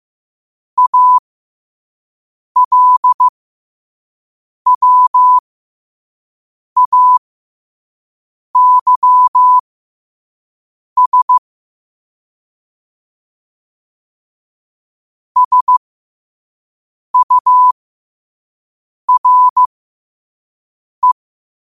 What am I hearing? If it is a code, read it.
Morse code, heard as a keyed tone, about 15 wpm, spaced as for 5 wpm, 1000 Hz. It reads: ALWAYS SURE
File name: qotdfiles / 5wpm